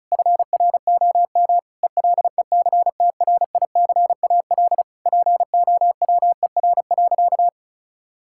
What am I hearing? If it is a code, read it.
Morse code, heard as a keyed tone, about 36 wpm, 700 Hz. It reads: FROM ELECTRICAL POWER.
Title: Sentences from Top 1000 Words - Review Part 6 - 35wpm